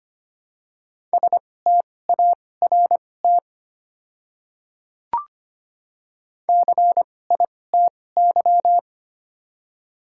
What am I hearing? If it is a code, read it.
Morse code, heard as a keyed tone, about 25 wpm, spaced as for 18 wpm, 700 Hz. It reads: START CITY
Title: 2x Character Spacing - Top 300 Words - 25wpm